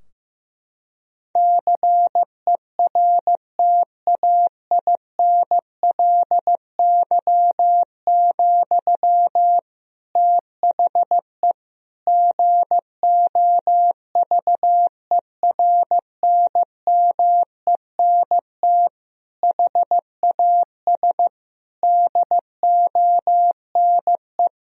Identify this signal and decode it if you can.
Morse code, heard as a keyed tone, about 15 wpm, 700 Hz. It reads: CERTAINLY, THE GOVERNMENT HAS DONE
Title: Sentences from Top 1000 Words - Review Part 3 - 15wpm